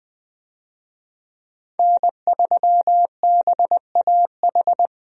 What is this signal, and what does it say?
Morse code, heard as a keyed tone, about 20 wpm, 700 Hz. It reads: N3BAH
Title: USA Call Signs - 5 Characters - Repeated 2x - 20wpm